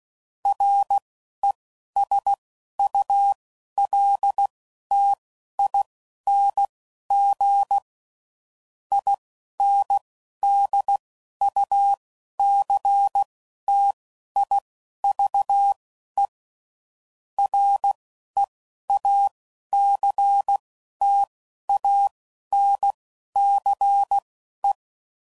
Morse code, heard as a keyed tone, about 16 wpm, spaced as for 11 wpm, 800 Hz. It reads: RESULTING INDUCTIVE REACTANCE
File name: CW Code Groups @10WPM / Technical Subjects / XL